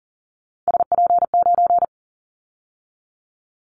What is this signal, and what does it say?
Morse code, heard as a keyed tone, about 40 wpm, 700 Hz. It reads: SP9
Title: Callsign Prefixes for Top 25 DXCC - 40wpm